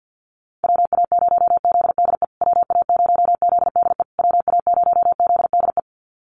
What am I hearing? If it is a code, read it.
Morse code, heard as a keyed tone, about 48 wpm, 700 Hz. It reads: WA0ZDE WA0ZDE WA0ZDE
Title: USA Call Signs Repeated 3x 50wpm